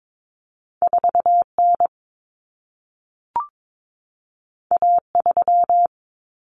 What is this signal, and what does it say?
Morse code, heard as a keyed tone, about 22 wpm, 700 Hz. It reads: 4N A3
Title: Common Cut Numbers - 2-Digit 22wpm